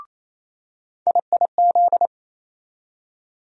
Morse code, heard as a keyed tone, about 28 wpm, 700 Hz. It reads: IIZ
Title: USA Call Sign Postfixes - 3 Letters 28wpm